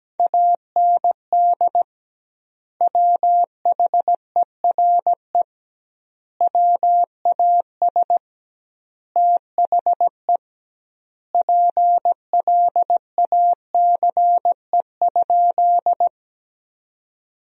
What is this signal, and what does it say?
Morse code, heard as a keyed tone, about 17 wpm, 700 Hz. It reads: AND WHERE WAS THE PLACE?
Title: 2x Word-Spacing - Sentences from Top 200 Words - 17wpm